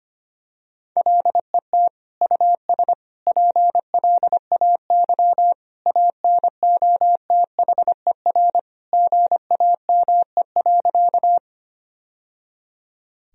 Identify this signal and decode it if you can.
Morse code, heard as a keyed tone, about 25 wpm, 700 Hz. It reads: LET US PLAY ANOTHER GAME.
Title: Sentences from Top 1000 Words - Review Part 3 - 25wpm